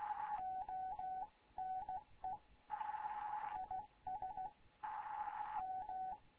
Morse code, heard as a keyed tone, about 16 wpm, 750 Hz, with radio noise. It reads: ONE IS M
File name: Morze-HA2MN / hangp / 2 11 GYAKORLAT